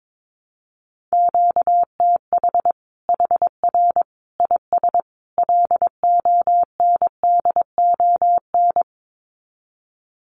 Morse code, heard as a keyed tone, about 22 wpm, 700 Hz. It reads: QTH HR IS LONDON
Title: Common QSO Phrases 22wpm